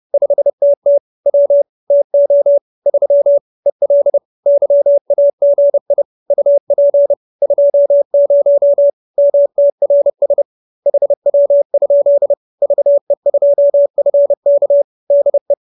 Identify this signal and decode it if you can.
Morse code, heard as a keyed tone, about 30 wpm, 570 Hz. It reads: 5TT W TO 3 EL YAGI UP 20 MTRS HW? VE2FK DE
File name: QSO314 30